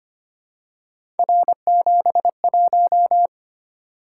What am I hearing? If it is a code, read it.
Morse code, heard as a keyed tone, about 25 wpm, 700 Hz. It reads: R71
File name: Three Letter-Number Combinations - 25wpm